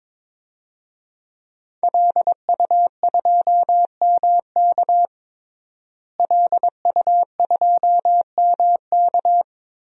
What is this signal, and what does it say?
Morse code, heard as a keyed tone, about 22 wpm, 700 Hz. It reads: LU2MK LU2MK
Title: DXCC-100 Call Signs - 5 Characters - Repeated 3x - 22wpm